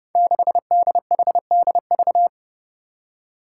Morse code, heard as a keyed tone, about 30 wpm, 700 Hz. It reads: 6DHDV
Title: Five Challenging Letter-Number Combinations 30wpm